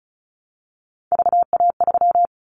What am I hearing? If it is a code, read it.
Morse code, heard as a keyed tone, about 36 wpm, 700 Hz. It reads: VA3
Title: Callsign Prefixes for Top 50 DXCC - 35wpm